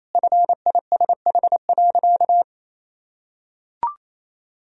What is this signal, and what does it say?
Morse code, heard as a keyed tone, about 28 wpm, 700 Hz. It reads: FISH.
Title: Sentences from Top 1000 Words - Review Part 3 - 28wpm